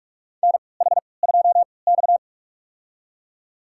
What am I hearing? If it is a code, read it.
Morse code, heard as a keyed tone, about 41 wpm, spaced as for 25 wpm, 700 Hz. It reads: NH2X
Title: USA Call Signs - 4 Characters - 3x Character Spacing - 45wpm